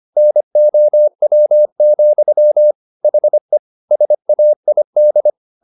Morse code, heard as a keyed tone, about 25 wpm, 600 Hz. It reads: NOW, HE SAID